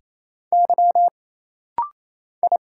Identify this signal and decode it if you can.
Morse code, heard as a keyed tone, about 27 wpm, 700 Hz. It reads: Y I
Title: Single Letters - Rapid-Fire 28wpm